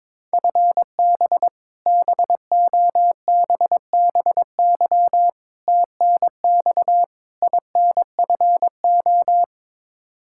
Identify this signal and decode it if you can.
Morse code, heard as a keyed tone, about 22 wpm, 700 Hz. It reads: FB BOBBY TNX INFO
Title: Practice QSO - Exchange 3 - More Info - 22wpm